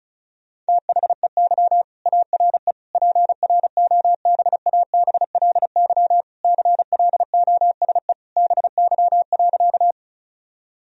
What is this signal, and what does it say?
Morse code, heard as a keyed tone, about 36 wpm, 700 Hz. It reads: THEY ARE PROBABLY CLOSE BY.
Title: Sentences from Top 1000 Words - Review Part 4 - 35wpm